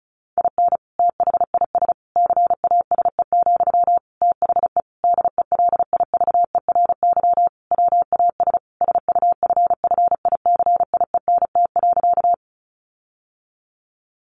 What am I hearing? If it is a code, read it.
Morse code, heard as a keyed tone, about 34 wpm, 700 Hz. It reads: IN THIS CASE, THE DELIVERY WAS SUFFICIENT.
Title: Sentences from Top 1500 Words 35wpm